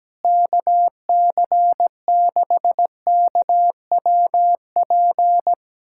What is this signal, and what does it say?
Morse code, heard as a keyed tone, about 17 wpm, 700 Hz. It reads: KC6KWP
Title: USA Callsign DE Callsign - 6 Characters - 17wpm